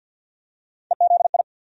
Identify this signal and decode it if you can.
Morse code, heard as a keyed tone, about 47 wpm, 700 Hz. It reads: EZI
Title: USA Call Sign Postfixes - 3 Letters 50wpm